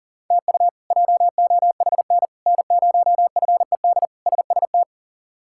Morse code, heard as a keyed tone, about 40 wpm, 700 Hz. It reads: TU JOHN N0FED SST
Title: K1USN SST QSO Practice - 40wpm